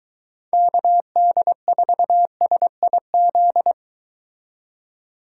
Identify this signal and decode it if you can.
Morse code, heard as a keyed tone, about 23 wpm, 700 Hz. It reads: KD4SIZ